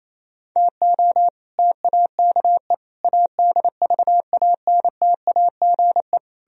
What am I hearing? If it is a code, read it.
Morse code, heard as a keyed tone, about 37 wpm, 700 Hz. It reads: TO TAKE ADVANTAGE